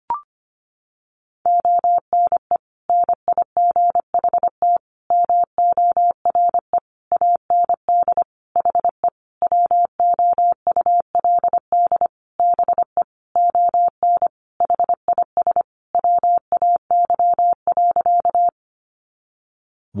Morse code, heard as a keyed tone, about 25 wpm, 700 Hz. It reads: ONE NIGHT MORE AND HE WOULD BE ON HIS WAY.
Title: Sentences from Top 300 Words 25wpm